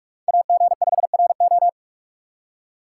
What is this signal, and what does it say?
Morse code, heard as a keyed tone, about 57 wpm, 700 Hz. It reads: AG5RO